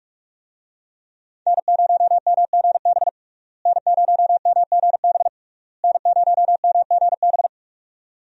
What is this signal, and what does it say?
Morse code, heard as a keyed tone, about 46 wpm, 700 Hz. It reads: N0MGB N0MGB N0MGB
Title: USA Call Signs - 5 Characters - Repeated 3x - 45wpm